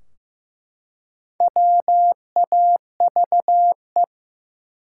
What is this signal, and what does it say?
Morse code, heard as a keyed tone, about 15 wpm, 700 Hz. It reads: WAVE
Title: Four Letter Words - 15wpm